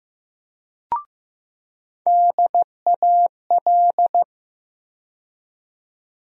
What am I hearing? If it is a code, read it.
Morse code, heard as a keyed tone, about 15 wpm, 700 Hz. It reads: DAL
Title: USA Call Sign Postfixes - 3 Letters 15wpm